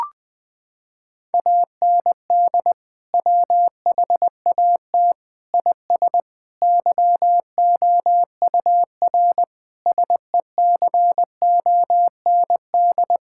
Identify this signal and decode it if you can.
Morse code, heard as a keyed tone, about 20 wpm, 700 Hz. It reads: AND WHAT IS YOUR SECOND
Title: Sentences from Top 1600 Words 20wpm